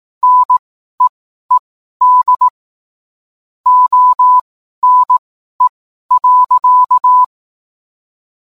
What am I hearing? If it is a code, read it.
Morse code, heard as a keyed tone, about 18 wpm, spaced as for 12 wpm, 1000 Hz. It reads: NEED ONE.